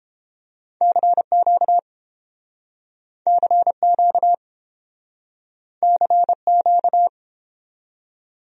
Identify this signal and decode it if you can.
Morse code, heard as a keyed tone, about 30 wpm, 700 Hz. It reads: CQ CQ CQ